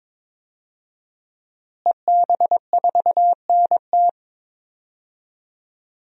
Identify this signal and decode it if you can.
Morse code, heard as a keyed tone, about 21 wpm, 700 Hz. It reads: EB4NT
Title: DXCC-100 Call Signs - 5 Characters - Repeated 2x - 22wpm